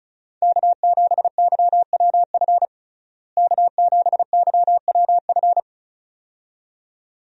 Morse code, heard as a keyed tone, about 35 wpm, 700 Hz. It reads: K7YWF K7YWF
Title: USA Call Signs - 5 Characters - Repeated 2x - 35wpm